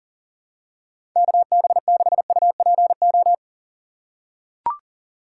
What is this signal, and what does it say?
Morse code, heard as a keyed tone, about 40 wpm, 700 Hz. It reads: KB6UPO
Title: USA Call Signs - 6 Characters - 8x Character Spacing - 40wpm